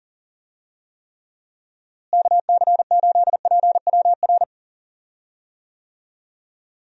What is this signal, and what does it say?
Morse code, heard as a keyed tone, about 40 wpm, 700 Hz. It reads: KC8PWR
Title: USA Call Signs - 6 Characters - Repeated 3x - 40wpm